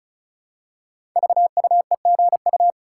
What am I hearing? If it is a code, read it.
Morse code, heard as a keyed tone, about 36 wpm, 700 Hz. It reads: VUEGU
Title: Code Groups - 5 Letters - 35wpm